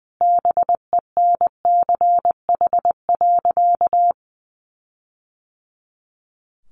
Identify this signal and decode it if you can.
Morse code, heard as a keyed tone, about 20 wpm, 700 Hz. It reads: BENCH.
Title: Sentences from Top 1400 Words 20wpm